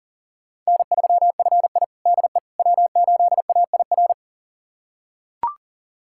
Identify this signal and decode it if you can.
Morse code, heard as a keyed tone, about 40 wpm, 700 Hz. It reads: N3FI DE W8AIR
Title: USA Callsign DE Callsign 40wpm